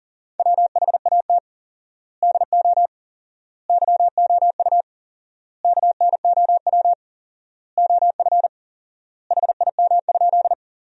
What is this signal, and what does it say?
Morse code, heard as a keyed tone, about 40 wpm, 700 Hz. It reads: WHAT DO YOU KNOW OF HIM?
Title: 4x Word-Spacing - Sentences from Top 100 Words - 40wpm